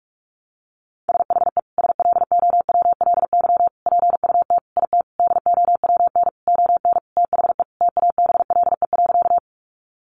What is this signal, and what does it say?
Morse code, heard as a keyed tone, about 44 wpm, 700 Hz. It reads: SHE SLOWLY PUT IT DOWN ON THE TABLE.
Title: Sentences from Top 1100 Words 45wpm